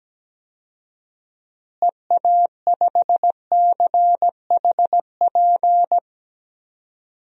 Morse code, heard as a keyed tone, about 17 wpm, 700 Hz. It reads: EA5CHP
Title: DXCC-100 Call Signs - 6 Characters - Repeated 2x - 17wpm